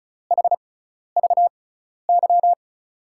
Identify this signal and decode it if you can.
Morse code, heard as a keyed tone, about 36 wpm, spaced as for 12 wpm, 700 Hz. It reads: HVY
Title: USA Call Signs - 6x Character Spacing - 35wpm